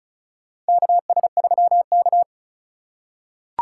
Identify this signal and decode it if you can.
Morse code, heard as a keyed tone, about 36 wpm, 700 Hz. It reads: KS3K
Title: USA Call Signs - 4 Characters - 3x Character Spacing - 35wpm